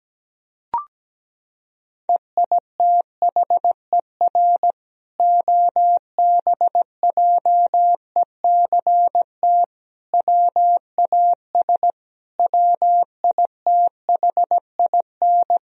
Morse code, heard as a keyed tone, about 17 wpm, 700 Hz. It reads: EITHER OBJECT WAS WITHIN
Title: Sentences from Top 1000 Words - Review Part 4 - 17wpm